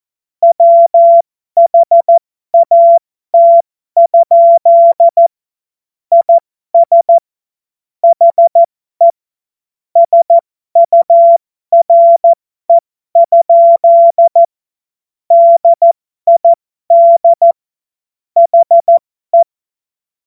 Morse code, heard as a keyed tone, about 14 wpm, spaced as for 12 wpm, 680 Hz. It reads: WHAT? IS HE SURE? DID HE
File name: Morsekurs / Witze